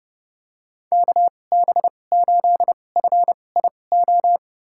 Morse code, heard as a keyed tone, about 30 wpm, spaced as for 22 wpm, 700 Hz. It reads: KB8FIO
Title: USA Call Signs - 2x Character Spacing - 30wpm